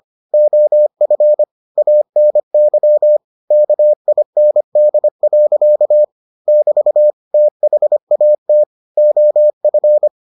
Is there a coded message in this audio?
Yes: Morse code from 0.3 to 10.1 s, heard as a keyed tone, about 25 wpm, 600 Hz. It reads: OF ANY KIND. <BT> THAT OF